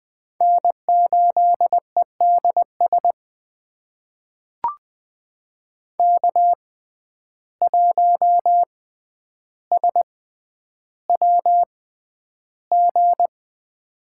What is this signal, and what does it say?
Morse code, heard as a keyed tone, about 20 wpm, spaced as for 10 wpm, 700 Hz. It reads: N8EDS K1SWG
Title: USA Call Signs - 5 Characters - 6x Character Spacing - 20wpm